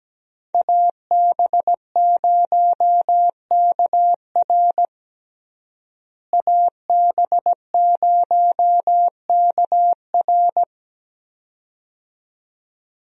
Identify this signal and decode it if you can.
Morse code, heard as a keyed tone, about 17 wpm, 700 Hz. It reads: AB0KR AB0KR
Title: USA Call Signs - 5 Characters - Repeated 2x - 17wpm